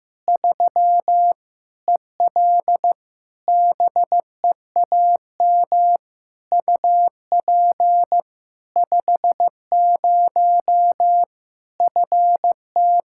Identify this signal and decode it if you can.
Morse code, heard as a keyed tone, about 16 wpm, 700 Hz. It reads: 3 EL BEAM UP 50 FT
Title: K1USN SST QSO Practice 15wpm